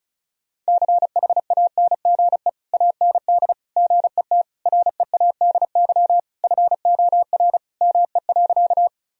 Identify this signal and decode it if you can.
Morse code, heard as a keyed tone, about 34 wpm, 700 Hz. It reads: CHANGE AND GET READY FOR ME.
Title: Sentences from Top 1000 Words - Review Part 1 - 35wpm